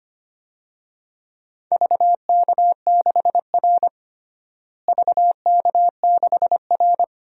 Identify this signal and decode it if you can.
Morse code, heard as a keyed tone, about 25 wpm, 700 Hz. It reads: VK6R VK6R
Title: DXCC-100 Call Signs - 4 Characters - Repeated 2x - 25wpm